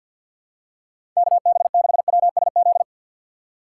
Morse code, heard as a keyed tone, about 47 wpm, 700 Hz. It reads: KB6WSZ
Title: USA Call Signs - 6 Characters - 4x Character Spacing - 50wpm